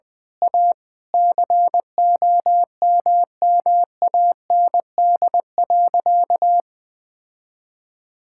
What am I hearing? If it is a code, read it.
Morse code, heard as a keyed tone, about 21 wpm, 700 Hz. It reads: A COMMAND.
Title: Sentences from Top 1600 Words 20wpm